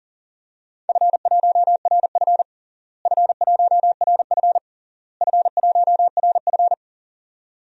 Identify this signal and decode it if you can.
Morse code, heard as a keyed tone, about 40 wpm, 700 Hz. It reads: F1RF F1RF F1RF
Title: DXCC-100 Call Signs - 4 Characters - Repeated 3x - 40wpm